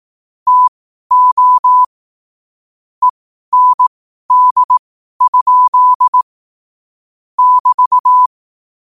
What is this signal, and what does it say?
Morse code, heard as a keyed tone, about 18 wpm, spaced as for 12 wpm, 1000 Hz. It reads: TO END? <BT>